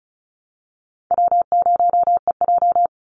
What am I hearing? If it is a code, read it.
Morse code, heard as a keyed tone, about 34 wpm, 700 Hz. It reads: W0EJ
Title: Easy Call Signs Repeated 2x - 35wpm